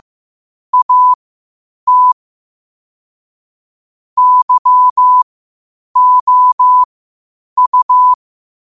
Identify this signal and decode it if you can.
Morse code, heard as a keyed tone, about 15 wpm, spaced as for 8 wpm, 1000 Hz. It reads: AT YOU
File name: qotdfiles / 8wpm